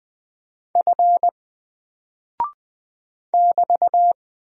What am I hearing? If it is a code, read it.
Morse code, heard as a keyed tone, about 20 wpm, 700 Hz. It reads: F <BT>
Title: Single Character (Letter:Number:Punctuation:Prosign) 20wpm